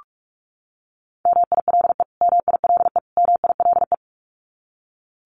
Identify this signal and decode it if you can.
Morse code, heard as a keyed tone, about 46 wpm, 700 Hz. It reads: MILE MILE MILE
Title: Top 300 Words Repeated 45wpm